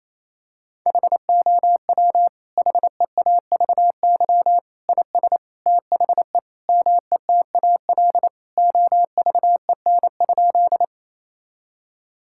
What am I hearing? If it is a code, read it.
Morse code, heard as a keyed tone, about 28 wpm, 700 Hz. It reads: HOW HEAVY IS THE METAL OVEN?
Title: Sentences from Top 1000 Words - Review Part 6 - 28wpm